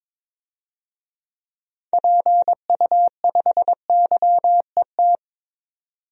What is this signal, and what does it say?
Morse code, heard as a keyed tone, about 22 wpm, 700 Hz. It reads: PU5YET
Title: DXCC-100 Call Signs - 6 Characters - 22wpm